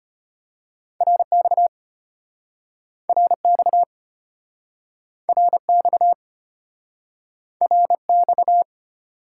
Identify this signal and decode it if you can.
Morse code, heard as a keyed tone, about 33 wpm, 700 Hz. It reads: RX RX RX RX